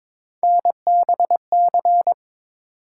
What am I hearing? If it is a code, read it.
Morse code, heard as a keyed tone, about 22 wpm, 700 Hz. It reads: NBC
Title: Sets of 5 Words Encore - New Word Drill - 22wpm